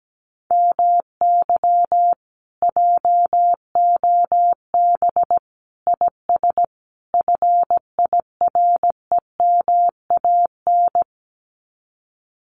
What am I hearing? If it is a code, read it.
Morse code, heard as a keyed tone, about 17 wpm, 700 Hz. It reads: MY JOB IS FIREMAN
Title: Common QSO Phrases Encore 17wpm